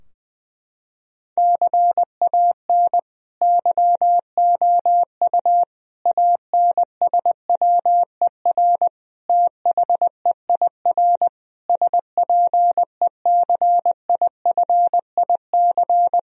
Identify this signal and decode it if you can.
Morse code, heard as a keyed tone, about 21 wpm, 700 Hz. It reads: CAN YOU ANSWER THEIR SPECIFIC
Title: Sentences from Top 1000 Words - Review Part 3 - 20wpm